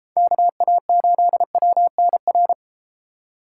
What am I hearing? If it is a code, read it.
Morse code, heard as a keyed tone, about 31 wpm, 700 Hz. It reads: KA8WNR